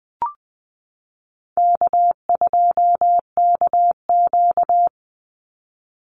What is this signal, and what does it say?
Morse code, heard as a keyed tone, about 20 wpm, 700 Hz. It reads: K2KQ
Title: Easy Call Signs Repeated 2x - 20wpm